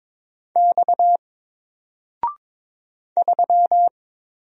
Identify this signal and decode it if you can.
Morse code, heard as a keyed tone, about 22 wpm, 700 Hz. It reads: X 3
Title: Single Letter-Number Alt 22wpm